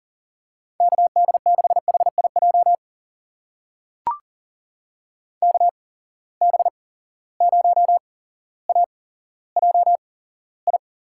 Morse code, heard as a keyed tone, about 40 wpm, spaced as for 17 wpm, 700 Hz. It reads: KD6HIJ KB0AJI
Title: USA Call Signs - 6 Characters - 8x Character Spacing - 40wpm